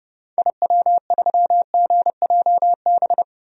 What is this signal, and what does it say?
Morse code, heard as a keyed tone, about 30 wpm, 700 Hz. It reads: IW3GJB
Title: DXCC-100 Call Signs - 6 Characters - Repeated 2x - 30wpm